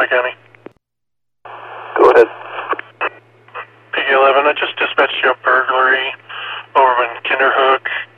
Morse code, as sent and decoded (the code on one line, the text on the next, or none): none